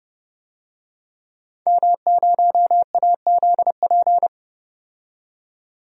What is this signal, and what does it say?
Morse code, heard as a keyed tone, about 30 wpm, 700 Hz. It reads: M0AZP
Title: DXCC-100 Call Signs - 5 Characters - 30wpm